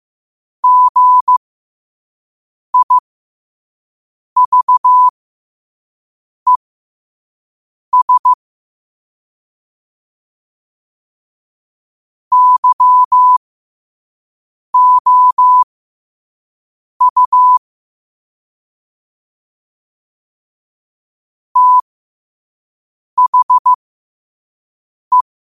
Morse code, heard as a keyed tone, about 15 wpm, spaced as for 5 wpm, 1000 Hz. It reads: GIVES YOU THE
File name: qotdfiles / 5wpm